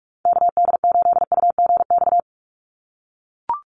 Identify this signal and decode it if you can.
Morse code, heard as a keyed tone, about 44 wpm, 700 Hz. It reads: KD8UGX
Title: USA Call Signs 45wpm